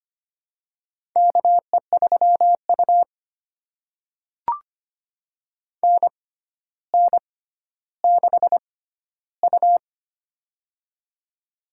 Morse code, heard as a keyed tone, about 25 wpm, spaced as for 13 wpm, 700 Hz. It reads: KE3U NN6U
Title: USA Call Signs - 4 Characters - 6x Character Spacing - 25wpm